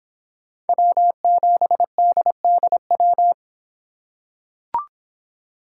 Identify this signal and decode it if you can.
Morse code, heard as a keyed tone, about 26 wpm, 700 Hz. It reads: W7DDW